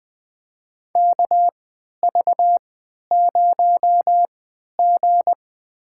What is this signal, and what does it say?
Morse code, heard as a keyed tone, about 20 wpm, spaced as for 11 wpm, 700 Hz. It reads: KV0G
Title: USA Call Signs - 4 Characters - 3x Character Spacing - 20wpm